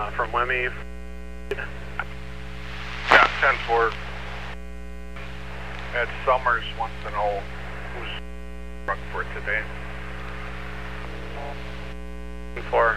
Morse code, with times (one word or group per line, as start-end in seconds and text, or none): none